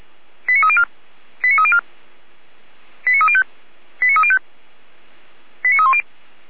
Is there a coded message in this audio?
Yes: five-tone selective call (ZVEI) from 0.5 to 6.0 s, heard as quick runs of five tones, tones of about 70 ms. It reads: 89304 89304 89305 89305 89310